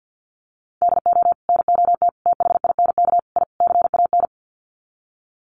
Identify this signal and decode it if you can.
Morse code, heard as a keyed tone, about 50 wpm, 700 Hz. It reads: DO NOT THINK I CAN
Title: Sets of 5 Words Encore2 50wpm